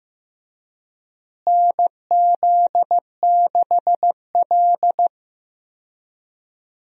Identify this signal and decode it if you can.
Morse code, heard as a keyed tone, about 15 wpm, 700 Hz. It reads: NZ6L
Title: USA Call Signs - 4 Characters - Repeated 2x - 15wpm